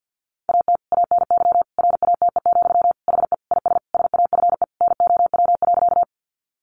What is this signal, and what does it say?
Morse code, heard as a keyed tone, about 50 wpm, 700 Hz. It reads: AT ANY RATE, HE IS SAFE NOW.
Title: Sentences from Top 900 Words 50wpm